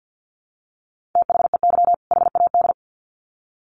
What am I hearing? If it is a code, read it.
Morse code, heard as a keyed tone, about 50 wpm, 700 Hz. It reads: THEY HAD
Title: Sets of 2 Words 50wpm